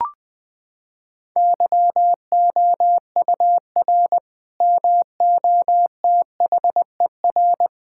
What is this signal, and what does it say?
Morse code, heard as a keyed tone, about 21 wpm, 700 Hz. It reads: YOUR MOTHER
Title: Sentences from Top 1600 Words 20wpm